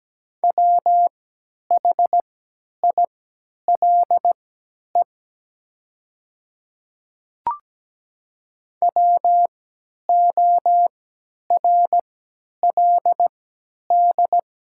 Morse code, heard as a keyed tone, about 17 wpm, spaced as for 10 wpm, 700 Hz. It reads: WHILE WORLD
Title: 3x Character Spacing - Top 300 Words - 17wpm